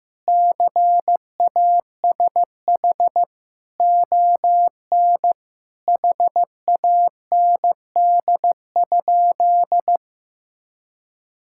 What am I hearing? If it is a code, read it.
Morse code, heard as a keyed tone, about 15 wpm, 700 Hz. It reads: CASH ON HAND?
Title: Sentences from Top 1000 Words - Review Part 5 - 15wpm